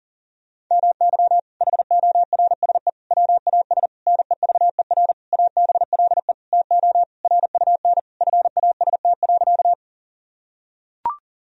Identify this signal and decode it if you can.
Morse code, heard as a keyed tone, about 40 wpm, 700 Hz. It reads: MY HORSE WAS NEVER ABLE TO RUN FAST.
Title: Sentences from Top 1000 Words - Review Part 2 - 40wpm